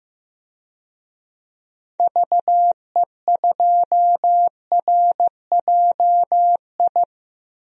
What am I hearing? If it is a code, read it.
Morse code, heard as a keyed tone, about 15 wpm, 700 Hz. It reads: VE2RJI
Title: DXCC-100 Call Signs - 6 Characters - 15wpm